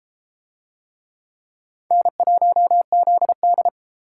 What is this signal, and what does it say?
Morse code, heard as a keyed tone, about 32 wpm, 700 Hz. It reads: N1ZD